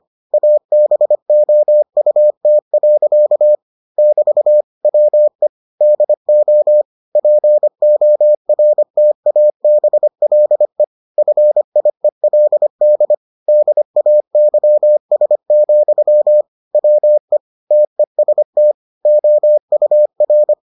Morse code, heard as a keyed tone, about 25 wpm, 600 Hz. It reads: ABOUT. <BT> WE DO PORTABLE FIELD DAYS, WE TEST OUR